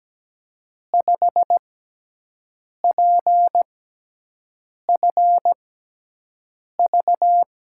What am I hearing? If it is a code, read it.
Morse code, heard as a keyed tone, about 17 wpm, spaced as for 6 wpm, 700 Hz. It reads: 5PFV
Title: USA Call Signs - 5 Characters - 6x Character Spacing - 17wpm